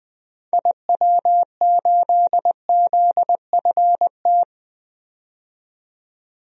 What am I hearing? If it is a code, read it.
Morse code, heard as a keyed tone, about 21 wpm, 700 Hz. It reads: IW8ZFT
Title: DXCC-100 Call Signs - 6 Characters - 20wpm